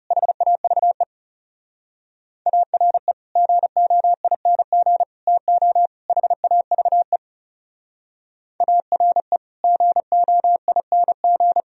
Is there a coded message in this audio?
Yes: Morse code from 0.1 to 11.6 s, heard as a keyed tone, about 40 wpm, 700 Hz. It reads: HAVE ARE GOING TO HAVE ARE GOING